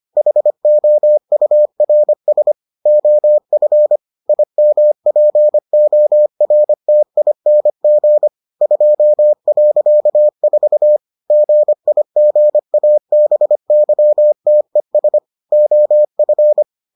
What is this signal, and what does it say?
Morse code, heard as a keyed tone, about 25 wpm, 600 Hz. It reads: HOURS OF IMPORTING 2.4 GIGABYTES OF